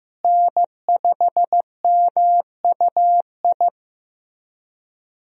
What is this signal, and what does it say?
Morse code, heard as a keyed tone, about 15 wpm, 700 Hz. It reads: N5MUI
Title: USA Call Signs - 5 Characters - Repeated 2x - 15wpm